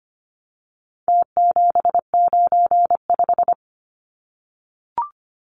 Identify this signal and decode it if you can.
Morse code, heard as a keyed tone, about 25 wpm, 700 Hz. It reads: T795
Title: Numbers - Contest Style - 4-Digit 25wpm